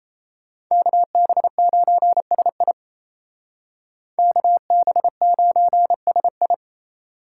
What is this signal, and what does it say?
Morse code, heard as a keyed tone, about 34 wpm, 700 Hz. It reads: KB9SI KB9SI
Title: Speed Racing - USA 2x2 Call Signs 28wpm